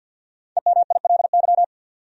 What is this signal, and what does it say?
Morse code, heard as a keyed tone, about 47 wpm, 700 Hz. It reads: EMILY
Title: USA Common Names - Max 5 Letters - 50wpm